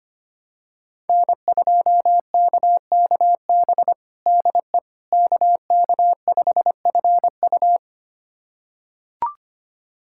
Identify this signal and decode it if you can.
Morse code, heard as a keyed tone, about 25 wpm, 700 Hz. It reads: N2KKB DE KK5FU
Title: USA Callsign DE Callsign 25wpm